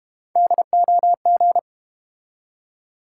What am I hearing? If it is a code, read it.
Morse code, heard as a keyed tone, about 30 wpm, 700 Hz. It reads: DOG
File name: Easy Words - Speed-Racing with Hints - 25wpm